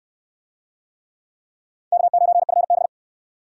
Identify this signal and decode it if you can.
Morse code, heard as a keyed tone, about 69 wpm, 700 Hz. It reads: K9FB